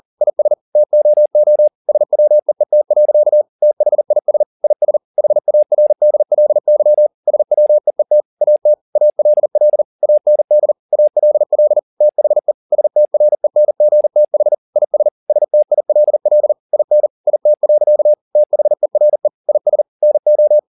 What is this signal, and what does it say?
Morse code, heard as a keyed tone, about 40 wpm, 600 Hz. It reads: IS TOO SWEET. THIS IS HARDLY SWEET AT ALL AND ALL THE STRENGTH IS STILL IN IT. THERE IS NO